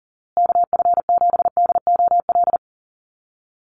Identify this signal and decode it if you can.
Morse code, heard as a keyed tone, about 40 wpm, 700 Hz. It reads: KF7DOL
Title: USA Call Signs Repeated 2x 40wpm